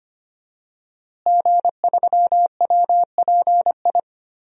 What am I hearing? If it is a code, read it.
Morse code, heard as a keyed tone, about 25 wpm, 700 Hz. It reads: G3WPI
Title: DXCC-100 Call Signs - 5 Characters - Repeated 2x - 25wpm